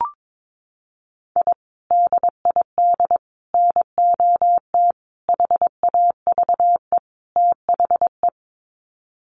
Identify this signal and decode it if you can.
Morse code, heard as a keyed tone, about 22 wpm, 700 Hz. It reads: I DID NOT HAVE THE
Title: Sets of 5 Words Encore 22wpm